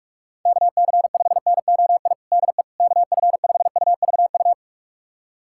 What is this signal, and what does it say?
Morse code, heard as a keyed tone, about 46 wpm, 700 Hz. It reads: KC5NOI DE KF5UVU
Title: USA Callsign DE Callsign - 6 Characters - 45wpm